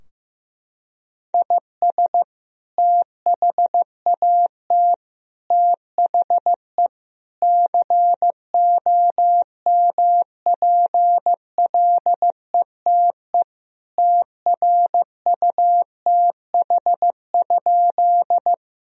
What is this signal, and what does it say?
Morse code, heard as a keyed tone, about 16 wpm, 700 Hz. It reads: IS THAT THE COMPLETE TRUTH?
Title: Sentences from Top 1000 Words - Review Part 2 - 15wpm